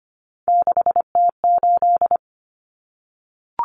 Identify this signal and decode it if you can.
Morse code, heard as a keyed tone, about 25 wpm, 700 Hz. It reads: 6T8
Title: Common Cut Numbers - 3-Digit 25wpm